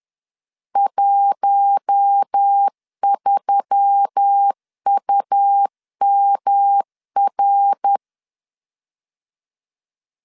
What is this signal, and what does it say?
Morse code, heard as a keyed tone, about 10 wpm, 800 Hz. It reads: 13UMR
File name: CW / DET / 1 6 1500C S B 10wpm 4137